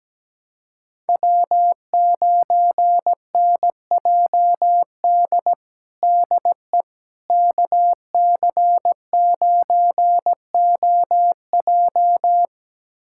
Morse code, heard as a keyed tone, about 17 wpm, 700 Hz. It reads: W9NJD DE KC9OJ
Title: USA Callsign DE Callsign - 5 Characters - 17wpm